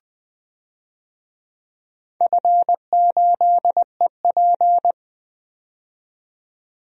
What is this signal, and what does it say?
Morse code, heard as a keyed tone, about 20 wpm, 700 Hz. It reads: F8EP
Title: DXCC-100 Call Signs - 4 Characters - 20wpm